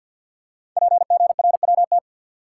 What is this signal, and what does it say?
Morse code, heard as a keyed tone, about 52 wpm, 700 Hz. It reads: PGRWT
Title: Code Groups - 5 Letters - 50wpm